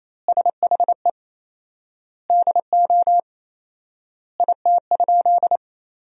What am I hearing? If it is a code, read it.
Morse code, heard as a keyed tone, about 27 wpm, 700 Hz. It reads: SHE DO IT?
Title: 4x Word-Spacing - Sentences from Top 100 Words - 28wpm